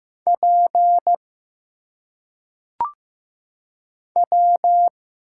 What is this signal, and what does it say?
Morse code, heard as a keyed tone, about 15 wpm, 700 Hz. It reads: P W
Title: Single Letters - Phonetic Pronunciation 15wpm